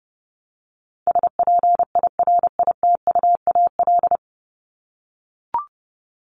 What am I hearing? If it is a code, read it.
Morse code, heard as a keyed tone, about 29 wpm, 700 Hz. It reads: SPIRITUAL
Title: Top 1501-1750 Words 30wpm